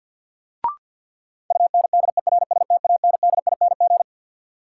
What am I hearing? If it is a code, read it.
Morse code, heard as a keyed tone, about 47 wpm, 700 Hz. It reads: UNDERSTANDING
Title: Sentences from Top 1400 Words - New Word Drill Alt - 50wpm